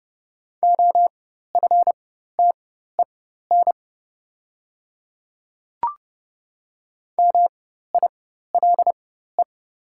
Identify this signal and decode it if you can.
Morse code, heard as a keyed tone, about 30 wpm, spaced as for 14 wpm, 690 Hz. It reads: OFTEN MILE
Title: 4x Character Spacing - Top 300 Words - 30wpm